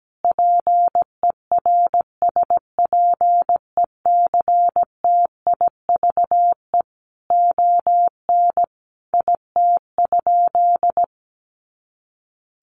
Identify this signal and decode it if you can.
Morse code, heard as a keyed tone, about 17 wpm, 700 Hz. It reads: PERSPECTIVE ON IT?
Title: Sentences from Top 1200 Words 17wpm